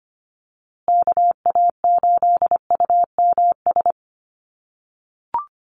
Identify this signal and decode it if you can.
Morse code, heard as a keyed tone, about 25 wpm, 700 Hz. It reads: KA8UMS
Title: USA Call Signs Repeated 2x 25wpm